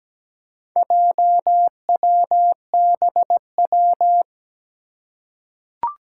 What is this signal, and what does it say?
Morse code, heard as a keyed tone, about 17 wpm, 700 Hz. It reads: JWBW
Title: Four Challenging Letter-Number Combinations 17wpm